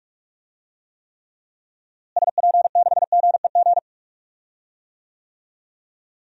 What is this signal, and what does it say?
Morse code, heard as a keyed tone, about 42 wpm, 700 Hz. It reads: SP6GEG
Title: DXCC-100 Call Signs - 6 Characters - Repeated 3x - 45wpm